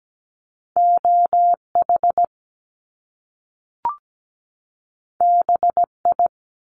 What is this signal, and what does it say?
Morse code, heard as a keyed tone, about 17 wpm, 700 Hz. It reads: OH BI
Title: Top Two Letter Combinations in Words 17wpm